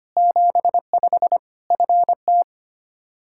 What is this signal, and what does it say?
Morse code, heard as a keyed tone, about 25 wpm, 700 Hz. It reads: 75 FT
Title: K1USN SST QSO Practice 25wpm